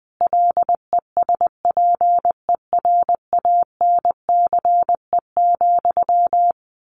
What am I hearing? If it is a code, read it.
Morse code, heard as a keyed tone, about 19 wpm, 700 Hz. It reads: LESPERANCE,
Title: A Sound Of Thunder 20wpm